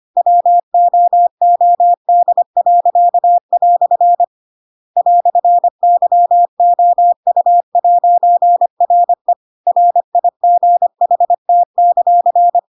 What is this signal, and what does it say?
Morse code, heard as a keyed tone, about 25 wpm, 700 Hz. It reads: WOOD." "YOU'RE RIGHT;
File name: Pinocchio / Pinocchio ch2 25 25 ik1hge